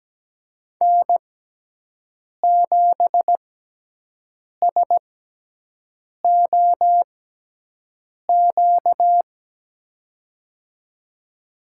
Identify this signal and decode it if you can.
Morse code, heard as a keyed tone, about 17 wpm, spaced as for 6 wpm, 700 Hz. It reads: N7SOQ
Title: USA Call Signs - 5 Characters - 6x Character Spacing - 17wpm